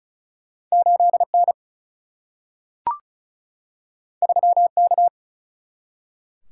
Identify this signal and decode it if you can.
Morse code, heard as a keyed tone, about 36 wpm, 700 Hz. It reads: 8 N 3 K
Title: Single-Letter Single-Number 35wpm